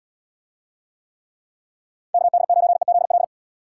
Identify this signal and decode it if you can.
Morse code, heard as a keyed tone, about 73 wpm, 700 Hz. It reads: KD9EZL